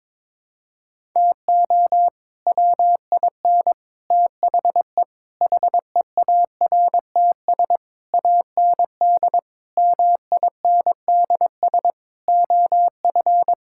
Morse code, heard as a keyed tone, about 22 wpm, 700 Hz. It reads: TO WIN THE HEARTS AND MINDS OF
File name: practice / 22wpm / Sets of 8 Words - 22wpm